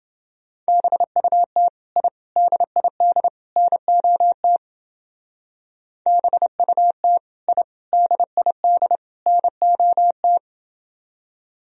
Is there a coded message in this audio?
Yes: Morse code from 0.7 to 10.4 s, heard as a keyed tone, about 28 wpm, 700 Hz. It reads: BUT I DID NOT BUT I DID NOT